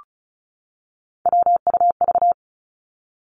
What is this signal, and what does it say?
Morse code, heard as a keyed tone, about 36 wpm, 700 Hz. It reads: WUV
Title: Three Challenging Letter-Number Combinations 35wpm